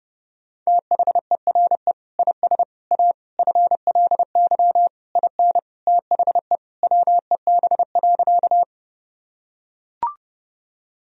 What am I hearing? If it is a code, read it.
Morse code, heard as a keyed tone, about 30 wpm, 700 Hz. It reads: THERE IS A FLY IN THE WEB.
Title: Sentences from Top 1000 Words - Review Part 4 - 30wpm